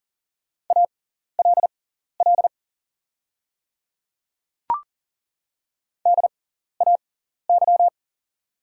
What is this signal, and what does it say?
Morse code, heard as a keyed tone, about 38 wpm, spaced as for 14 wpm, 690 Hz. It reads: ALL DAY
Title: 6x Character Spacing - Top 100 Words - 40wpm